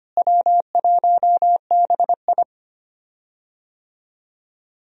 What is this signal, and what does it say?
Morse code, heard as a keyed tone, about 25 wpm, 700 Hz. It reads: W1BI
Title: USA Call Signs - 4 Characters - Repeated 3x - 25wpm